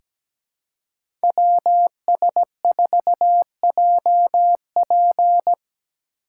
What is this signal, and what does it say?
Morse code, heard as a keyed tone, about 17 wpm, 700 Hz. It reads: WS4JP
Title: USA Call Signs - 5 Characters - 6x Character Spacing - 17wpm